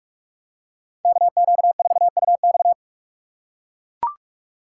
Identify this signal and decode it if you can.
Morse code, heard as a keyed tone, about 40 wpm, 700 Hz. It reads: KQ4UX